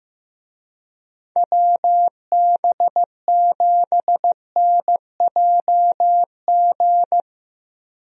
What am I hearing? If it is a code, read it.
Morse code, heard as a keyed tone, about 15 wpm, 700 Hz. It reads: WB7NJG
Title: USA Call Signs - 6 Characters - Repeated 3x - 15wpm